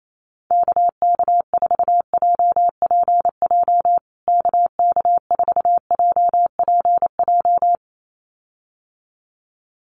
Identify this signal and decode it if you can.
Morse code, heard as a keyed tone, about 28 wpm, 700 Hz. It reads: KK4JPJ KK4JPJ
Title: USA Call Signs Repeated 2x 28wpm